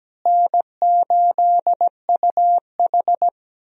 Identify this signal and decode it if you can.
Morse code, heard as a keyed tone, about 17 wpm, 700 Hz. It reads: N8UH
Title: USA Call Signs - 4 Characters - Repeated 2x - 17wpm